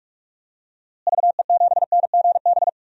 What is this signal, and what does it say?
Morse code, heard as a keyed tone, about 44 wpm, 690 Hz. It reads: VE7NGB
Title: DXCC-100 Call Signs - 6 Characters - Repeated 2x - 45wpm